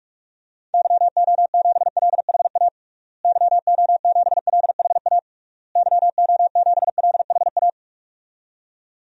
Morse code, heard as a keyed tone, about 41 wpm, 700 Hz. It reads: YO7LHA YO7LHA YO7LHA
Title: DXCC-100 Call Signs - 6 Characters - Repeated 3x - 45wpm